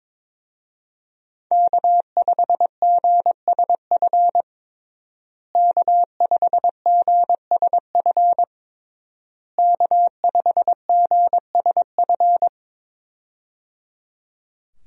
Morse code, heard as a keyed tone, about 22 wpm, 700 Hz. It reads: K5GSF K5GSF K5GSF
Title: USA Call Signs - 5 Characters - Repeated 3x - 22wpm